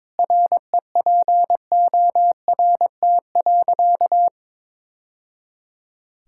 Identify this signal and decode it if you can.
Morse code, heard as a keyed tone, about 21 wpm, 700 Hz. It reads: REPORT.
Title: Sentences from Top 1000 Words - Review Part 6 - 22wpm